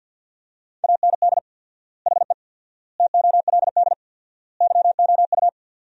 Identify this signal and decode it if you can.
Morse code, heard as a keyed tone, about 53 wpm, 700 Hz. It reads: AND HE TOLD YOU
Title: 4x Word-Spacing - Sentences from Top 500 Words - 50wpm